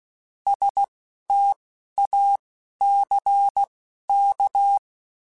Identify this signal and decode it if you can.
Morse code, heard as a keyed tone, about 16 wpm, spaced as for 11 wpm, 800 Hz. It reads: STACK
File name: Alphabetical / Alphabetical mp3s / Ss @10